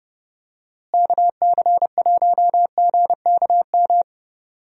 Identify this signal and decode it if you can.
Morse code, heard as a keyed tone, about 29 wpm, 700 Hz. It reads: KC1GKM
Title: USA Call Signs - 6 Characters - Repeated 2x - 30wpm